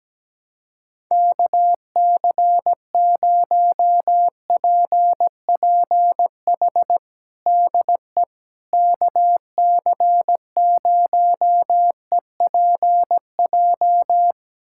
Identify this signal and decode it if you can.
Morse code, heard as a keyed tone, about 17 wpm, 700 Hz. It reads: KC0PPH DE KC0EPJ
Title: USA Callsign DE Callsign 17wpm